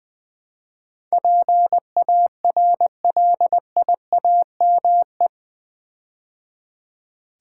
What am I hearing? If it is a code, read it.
Morse code, heard as a keyed tone, about 21 wpm, 700 Hz. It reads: PARLIAME
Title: Build A Word - 4 Letters At A Time - 10-Letter Words - 20wpm